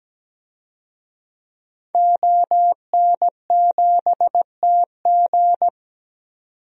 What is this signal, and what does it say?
Morse code, heard as a keyed tone, about 17 wpm, 700 Hz. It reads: ON7TG
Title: DXCC-100 Call Signs - 5 Characters - 17wpm